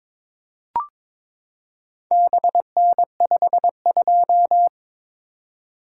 Morse code, heard as a keyed tone, about 22 wpm, 700 Hz. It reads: BN52
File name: practice / 22wpm / Four Letter-Number Combinations - 22wpm